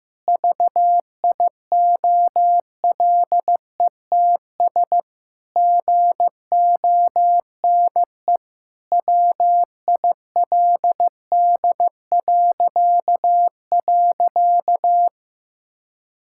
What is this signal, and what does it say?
Morse code, heard as a keyed tone, about 16 wpm, 700 Hz. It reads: VIOLETS GONE WILD..
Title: Pangram Challenge v2 - 15wpm